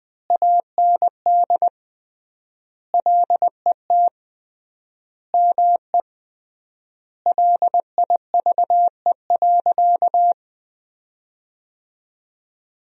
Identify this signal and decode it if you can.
Morse code, heard as a keyed tone, about 21 wpm, 700 Hz. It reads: AND LET ME LIVE.
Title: 3x Word-Spacing - Sentences from Top 300 Words - 20wpm